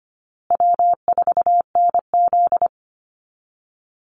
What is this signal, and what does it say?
Morse code, heard as a keyed tone, about 25 wpm, 700 Hz. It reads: W4NZ
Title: Easy Call Signs Repeated 3x - 25wpm